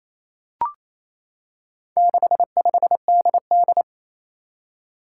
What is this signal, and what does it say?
Morse code, heard as a keyed tone, about 27 wpm, 700 Hz. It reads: 65DD
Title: Four Challenging Letter-Number Combinations 28wpm